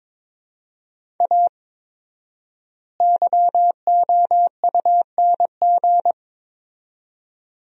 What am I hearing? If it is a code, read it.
Morse code, heard as a keyed tone, about 22 wpm, 700 Hz. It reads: A YOUNG
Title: 4x Word-Spacing - Sentences from Top 400 Words - 22wpm